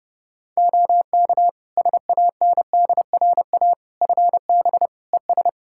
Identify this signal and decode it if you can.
Morse code, heard as a keyed tone, about 30 wpm, 700 Hz. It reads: OK SANDRA FB ES
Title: Practice QSO - Exchange 2 - Info - 30wpm